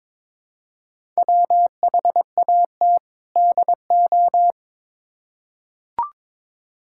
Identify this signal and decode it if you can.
Morse code, heard as a keyed tone, about 22 wpm, 700 Hz. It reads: WHAT DO
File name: Build A Phrase - 2-Word Phrases - 22wpm